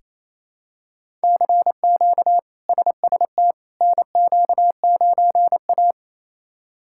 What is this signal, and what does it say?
Morse code, heard as a keyed tone, about 28 wpm, 700 Hz. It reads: CQ SST NQ9A
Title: K1USN SST QSO Practice - 28wpm